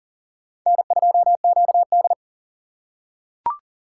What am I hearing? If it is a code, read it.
Morse code, heard as a keyed tone, about 40 wpm, 700 Hz. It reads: N2QD
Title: USA Call Signs - 4 Characters - 4x Character Spacing - 40wpm